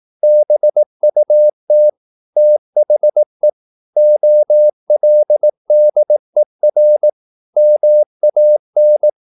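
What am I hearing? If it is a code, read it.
Morse code, heard as a keyed tone, about 18 wpm, 600 Hz. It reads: BUT THE OLDER MAN
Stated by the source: Morse practice at 18 wpm, English text, from CW med Gnister.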